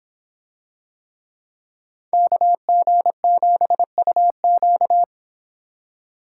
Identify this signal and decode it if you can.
Morse code, heard as a keyed tone, about 24 wpm, 700 Hz. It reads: KG7UQ